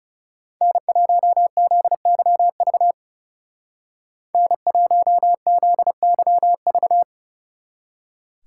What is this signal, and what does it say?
Morse code, heard as a keyed tone, about 35 wpm, 700 Hz. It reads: N1ZYV N1ZYV
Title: Speed Racing - USA Call Signs 30wpm